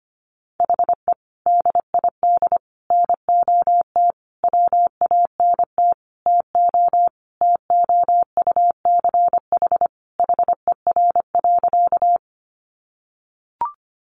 Morse code, heard as a keyed tone, about 25 wpm, 700 Hz. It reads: HE DID NOT WANT TO TOUCH HER.
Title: Sentences from Top 1200 Words 25wpm